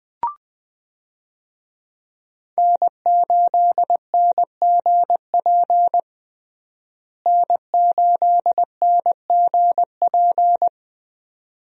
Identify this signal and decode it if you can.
Morse code, heard as a keyed tone, about 21 wpm, 700 Hz. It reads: N8NGP N8NGP
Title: USA Call Signs - 5 Characters - Repeated 2x - 20wpm